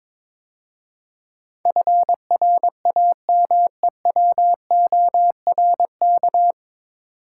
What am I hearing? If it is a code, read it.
Morse code, heard as a keyed tone, about 22 wpm, 700 Hz. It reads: FRAMEWORK
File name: 9 Letter Words - First 3 Letters Given - 22wpm